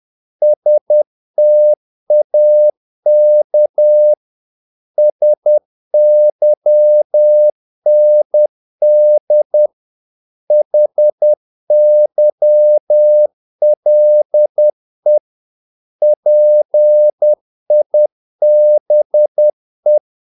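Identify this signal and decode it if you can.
Morse code, heard as a keyed tone, about 10 wpm, 600 Hz. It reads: STAK SYND HYLE PIBE
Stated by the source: Morse practice of short Danish words at 10 wpm, from CW med Gnister.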